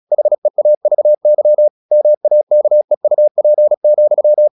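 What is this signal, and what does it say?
Morse code, heard as a keyed tone, about 36 wpm, 600 Hz. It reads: HEAVY MAKEUP,
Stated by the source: Morse practice at 36 wpm, text of Ken Follett's novel Never, from CW med Gnister.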